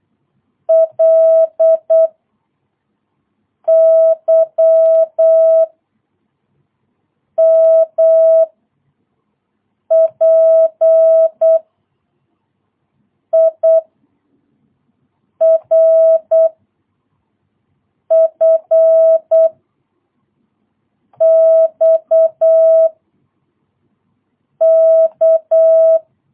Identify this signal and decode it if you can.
Morse code, heard as a keyed tone, about 8 wpm, 650 Hz. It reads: L Y M P I R F X K
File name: jh8cbh / 2022cw